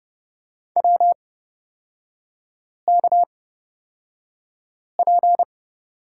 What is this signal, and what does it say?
Morse code, heard as a keyed tone, about 30 wpm, spaced as for 5 wpm, 700 Hz. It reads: WKP
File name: Lesson 37e - Letter x - Three Characters 30wpm